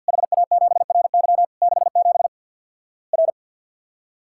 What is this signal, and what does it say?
Morse code, heard as a keyed tone, about 50 wpm, 700 Hz. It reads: HA7RY 67 R
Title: Worked All Europe DX Contest - QTC Exchange with Repeats - 50wpm